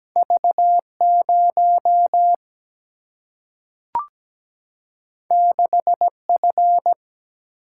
Morse code, heard as a keyed tone, about 17 wpm, 700 Hz. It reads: V 0 6 F
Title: Single-Letter Single-Number 17wpm